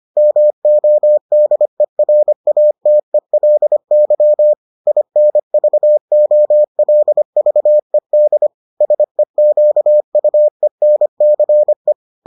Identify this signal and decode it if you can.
Morse code, heard as a keyed tone, about 25 wpm, 600 Hz. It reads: MODERATELY INVOLVED SEQUENCE